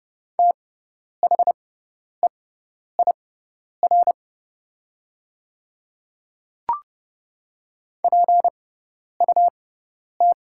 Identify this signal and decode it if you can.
Morse code, heard as a keyed tone, about 30 wpm, spaced as for 10 wpm, 690 Hz. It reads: THEIR PUT
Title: 6x Character Spacing - Top 200 Words - 30wpm